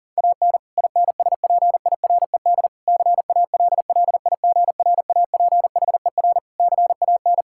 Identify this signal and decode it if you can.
Morse code, heard as a keyed tone, about 40 wpm, 700 Hz. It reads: AN INSPIRED CALLIGRAPHER CAN
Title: Pangram Challenge v2 - 40wpm